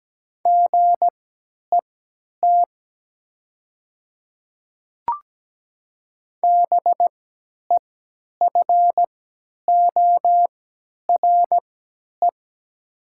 Morse code, heard as a keyed tone, about 17 wpm, spaced as for 10 wpm, 700 Hz. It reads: GET BEFORE
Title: 3x Character Spacing - Top 300 Words - 17wpm